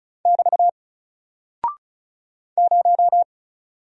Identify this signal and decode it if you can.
Morse code, heard as a keyed tone, about 34 wpm, 700 Hz. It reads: <BT> 0
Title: Single Character (Letter:Number:Punctuation:Prosign) Alt 35wpm